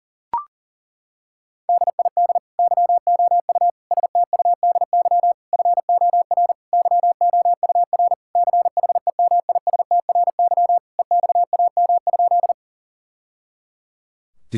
Morse code, heard as a keyed tone, about 40 wpm, 700 Hz. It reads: DID YOU STUDY FOR YOUR CHEMISTRY EXAM?
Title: Sentences from Top 1000 Words - Review Part 6 - 40wpm